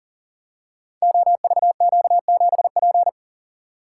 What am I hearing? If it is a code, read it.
Morse code, heard as a keyed tone, about 40 wpm, 700 Hz. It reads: OVQ7P
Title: Code Groups - 5 Letter:Numbers - 40wpm